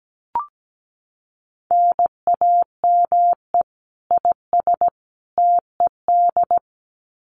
Morse code, heard as a keyed tone, about 17 wpm, 700 Hz. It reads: NAME IS TED
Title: Common QSO Phrases 17wpm